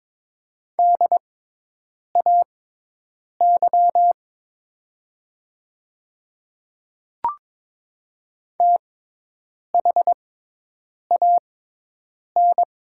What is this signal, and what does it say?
Morse code, heard as a keyed tone, about 22 wpm, spaced as for 8 wpm, 700 Hz. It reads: DAY THAN
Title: 6x Character Spacing - Top 100 Words - 22wpm